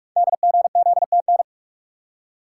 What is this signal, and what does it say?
Morse code, heard as a keyed tone, about 44 wpm, 700 Hz. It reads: DGZTN
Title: Code Groups - 5 Letter:Numbers - 45wpm